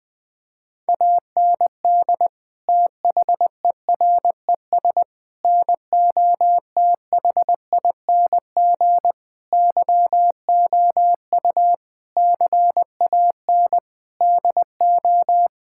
Morse code, heard as a keyed tone, about 21 wpm, 700 Hz. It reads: AND THERES NOTHING YOU CAN DO
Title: Sets of 8 Words - 20wpm